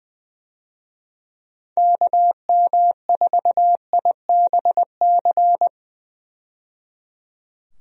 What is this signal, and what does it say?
Morse code, heard as a keyed tone, about 21 wpm, 700 Hz. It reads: KM4IBC
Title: USA Call Signs - 6 Characters - 20wpm